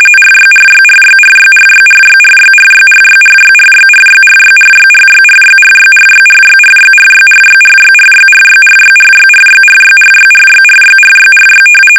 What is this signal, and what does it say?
RTTY, about 45.45 baud. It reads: RYRYRYRYRYRYRYRYRYRYRYRYRYRYRYRYRYRYRYRYRYRYRYRYRYRYRYRYRYRYRYRYRYRY
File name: rtty / RTTY Test 850